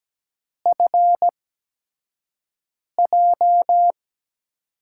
Morse code, heard as a keyed tone, about 17 wpm, spaced as for 5 wpm, 700 Hz. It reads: FJ
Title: USA Call Signs - 8x Character Spacing - 17wpm